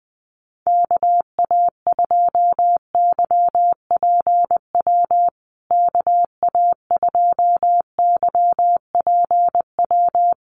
Morse code, heard as a keyed tone, about 20 wpm, 700 Hz. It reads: KA2YPW KA2YPW
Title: USA Call Signs Repeated 3x 20wpm